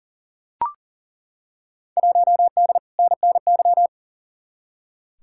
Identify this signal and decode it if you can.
Morse code, heard as a keyed tone, about 40 wpm, 700 Hz. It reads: 1D NNY
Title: Field Day Exchanges - 40wpm